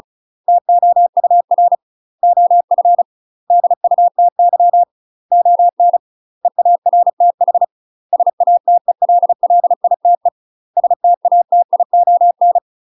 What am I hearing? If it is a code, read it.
Morse code, heard as a keyed tone, about 35 wpm, 700 Hz. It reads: TOUR OF DUTY ON EARTH SATELLITE STATION
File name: UK MASTER QSOs TRAINING/35WPM